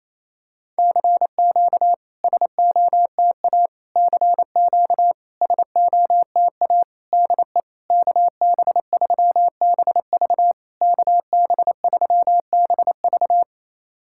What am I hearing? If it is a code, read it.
Morse code, heard as a keyed tone, about 28 wpm, 700 Hz. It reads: CQ SOTA CQ SOTA DE KB3BV KB3BV
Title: SOTA QSO Practice 28wpm